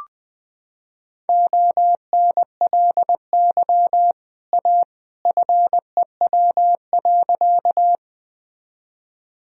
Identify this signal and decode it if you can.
Morse code, heard as a keyed tone, about 21 wpm, 700 Hz. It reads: ONLY A FEW.
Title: Sentences from Top 1000 Words - Review Part 1 - 20wpm